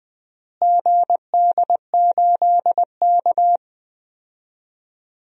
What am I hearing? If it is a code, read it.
Morse code, heard as a keyed tone, about 20 wpm, 700 Hz. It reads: GD8K
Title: DXCC-100 Call Signs - 4 Characters - Repeated 2x - 20wpm